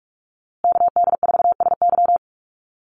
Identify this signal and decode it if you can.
Morse code, heard as a keyed tone, about 44 wpm, 700 Hz. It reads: KD4SY
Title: USA Call Signs Repeated 3x 45wpm